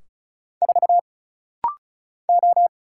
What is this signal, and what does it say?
Morse code, heard as a keyed tone, about 33 wpm, 700 Hz. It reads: 4 O
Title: Single Letter-Number Alt - Rapid-Fire 35wpm